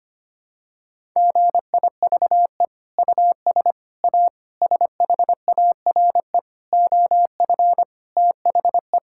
Morse code, heard as a keyed tone, about 25 wpm, 700 Hz. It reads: GIVE US A SHARE OF THE
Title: Sentences from Top 1000 Words - Review Part 6 - 25wpm